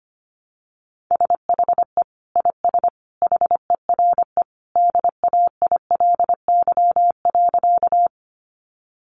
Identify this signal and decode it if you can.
Morse code, heard as a keyed tone, about 25 wpm, 700 Hz. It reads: SHE IS HERE DAILY.
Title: Sentences from Top 1100 Words 25wpm